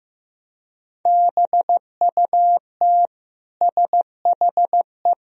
Morse code, heard as a keyed tone, about 15 wpm, 700 Hz. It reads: BUT SHE
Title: Sentences from Top 1000 Words - Review Part 1 - 15wpm